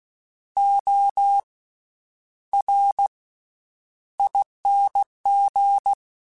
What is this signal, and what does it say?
Morse code, heard as a keyed tone, about 16 wpm, spaced as for 9 wpm, 800 Hz. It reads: ORING
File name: Special INGs-mp3s / Special INGs-mp3s / Slow ing @15